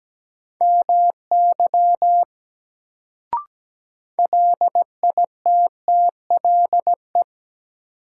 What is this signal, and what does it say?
Morse code, heard as a keyed tone, about 17 wpm, 700 Hz. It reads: MY LITTLE
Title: Sentences from Top 200 Words - New Word Drill - 17wpm